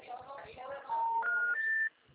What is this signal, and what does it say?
Special information tone, with noise.